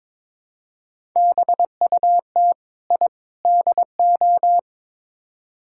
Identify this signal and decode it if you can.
Morse code, heard as a keyed tone, about 22 wpm, 700 Hz. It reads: BUT I DO
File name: practice / 22wpm / Build A Phrase - 5-Word Phrases - 22wpm